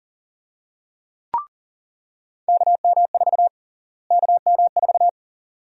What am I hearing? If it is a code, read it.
Morse code, heard as a keyed tone, about 41 wpm, 700 Hz. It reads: KM4 KM4
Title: USA Call Sign Prefixes - Repeated 2x - 40wpm